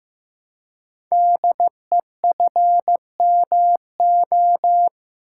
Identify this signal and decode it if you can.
Morse code, heard as a keyed tone, about 15 wpm, 700 Hz. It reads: DEFMO
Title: Code Groups - 5 Letters - 15wpm